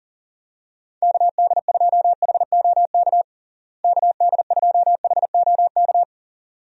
Morse code, heard as a keyed tone, about 40 wpm, 700 Hz. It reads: KD2HOK KD2HOK
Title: USA Call Signs - 6 Characters - Repeated 2x - 40wpm